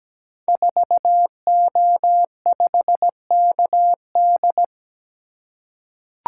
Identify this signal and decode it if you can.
Morse code, heard as a keyed tone, about 17 wpm, 700 Hz. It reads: 4O5KD
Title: Code Groups - 5 Letter:Numbers - 17wpm